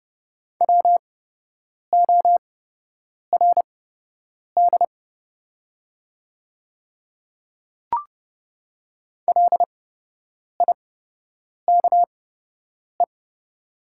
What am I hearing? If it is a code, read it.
Morse code, heard as a keyed tone, about 30 wpm, spaced as for 8 wpm, 700 Hz. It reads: WORD LIKE
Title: 8x Character Spacing - Top 100 Words - 30wpm